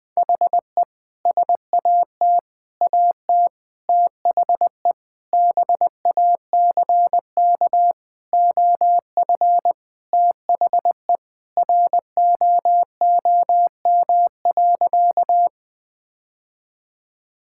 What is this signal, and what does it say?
Morse code, heard as a keyed tone, about 20 wpm, 700 Hz. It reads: HE SAT AT THE BACK OF THE ROOM.
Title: Sentences with 3-Word Phrases - Spoken Beginning - 20wpm